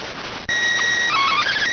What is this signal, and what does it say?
SSTV (slow-scan television), PD 290.